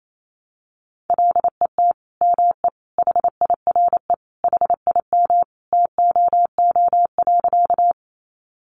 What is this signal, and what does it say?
Morse code, heard as a keyed tone, about 28 wpm, 700 Hz. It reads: LET ME HIRE HIM TOO.
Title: Sentences from Top 1400 Words 28wpm